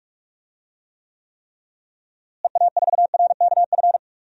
Morse code, heard as a keyed tone, about 46 wpm, 700 Hz. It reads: EA4RKF
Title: DXCC-100 Call Signs - 6 Characters - Repeated 3x - 45wpm